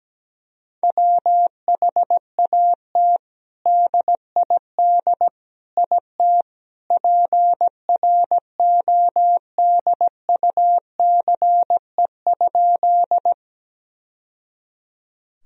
Morse code, heard as a keyed tone, about 17 wpm, 700 Hz. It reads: WHAT DID IT PRODUCE?
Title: Sentences from Top 1000 Words - Review Part 3 - 17wpm